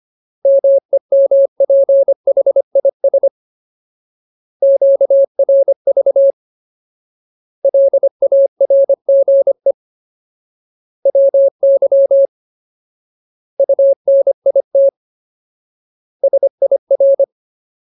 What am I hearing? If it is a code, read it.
Morse code, heard as a keyed tone, about 25 wpm, 550 Hz. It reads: MEMPHIS QRV LARGE WY UNIT SIR
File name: Ultimate Word List / 25WPM Ultimate